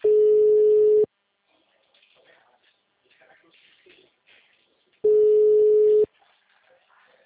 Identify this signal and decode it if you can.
Ringback tone (Europe, 425 Hz), with speech.